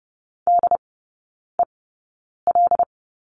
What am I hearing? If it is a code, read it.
Morse code, heard as a keyed tone, about 30 wpm, spaced as for 9 wpm, 700 Hz. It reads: DEL
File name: Lesson 16e - Letter c - Three Characters 30wpm